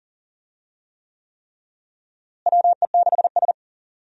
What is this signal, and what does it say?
Morse code, heard as a keyed tone, about 40 wpm, 700 Hz. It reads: WE6S
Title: USA Call Signs - 4 Characters - Repeated 3x - 40wpm